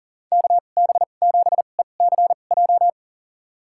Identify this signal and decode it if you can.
Morse code, heard as a keyed tone, about 40 wpm, spaced as for 29 wpm, 700 Hz. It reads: KB7ECJ
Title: USA Call Signs - 2x Character Spacing - 40wpm